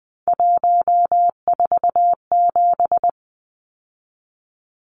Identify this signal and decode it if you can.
Morse code, heard as a keyed tone, about 20 wpm, 700 Hz. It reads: 147
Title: Numbers - 3-Digit 20wpm